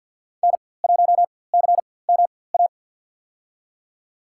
Morse code, heard as a keyed tone, about 53 wpm, spaced as for 24 wpm, 700 Hz. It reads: N1CMA
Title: USA Call Signs - 4x Character Spacing - 50wpm